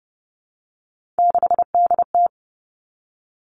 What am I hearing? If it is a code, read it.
Morse code, heard as a keyed tone, about 30 wpm, 700 Hz. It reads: 6DT
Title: Three Challenging Letter-Number Combinations 30wpm